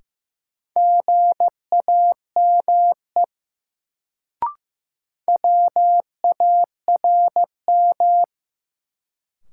Morse code, heard as a keyed tone, about 15 wpm, 700 Hz. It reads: GAME WARM
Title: Sentences from Top 500 Words - New Word Drill - 15wpm